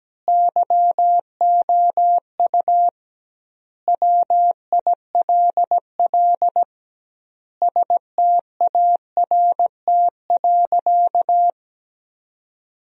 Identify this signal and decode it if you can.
Morse code, heard as a keyed tone, about 17 wpm, 700 Hz. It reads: YOU WILL START.
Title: 2x Word-Spacing - Sentences from Top 300 Words - 17wpm